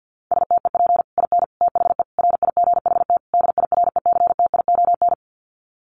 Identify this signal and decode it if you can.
Morse code, heard as a keyed tone, about 50 wpm, 700 Hz. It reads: STEP IN THE RIGHT DIRECTION
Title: Sets of 5 Words Encore 50wpm